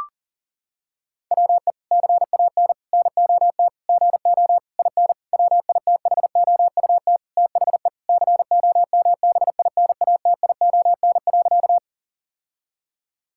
Morse code, heard as a keyed tone, about 37 wpm, 700 Hz. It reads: WE CAN NOT GO IN WITHOUT THE COMBINATION.
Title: Sentences from Top 1000 Words - Review Part 6 - 40wpm